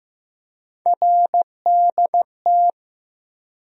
Morse code, heard as a keyed tone, about 15 wpm, 700 Hz. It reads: RDT
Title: USA Call Sign Postfixes - 3 Letters 15wpm